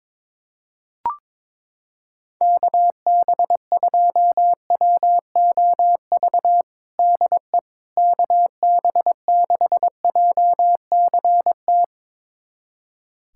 Morse code, heard as a keyed tone, about 22 wpm, 700 Hz. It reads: KB2WOV DE KB6JCT
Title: USA Callsign DE Callsign - 6 Characters - 22wpm